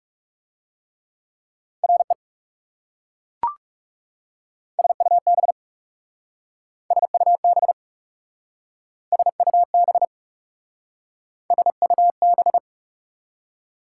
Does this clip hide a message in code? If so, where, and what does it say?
Yes: Morse code from 1.8 to 12.6 s, heard as a keyed tone, about 38 wpm, 700 Hz. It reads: RE SUB SUB SUB SUB